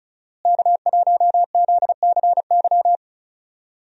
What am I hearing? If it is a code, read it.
Morse code, heard as a keyed tone, about 36 wpm, 700 Hz. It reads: K1ZCY
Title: USA Call Signs - 5 Characters - Repeated 3x - 35wpm